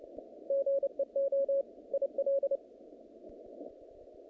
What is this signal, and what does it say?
Morse code, heard as a keyed tone, about 29 wpm, 550 Hz. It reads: GEO IL
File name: contests / naqp-cw-k5tr-op-k5pi- -2200